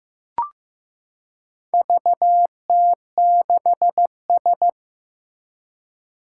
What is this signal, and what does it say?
Morse code, heard as a keyed tone, about 15 wpm, 700 Hz. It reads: VT6S
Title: Four Challenging Letter-Number Combinations 15wpm